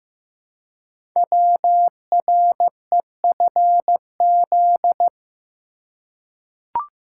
Morse code, heard as a keyed tone, about 15 wpm, 700 Hz. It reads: WREFZ
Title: Code Groups - 5 Letter:Numbers - 15wpm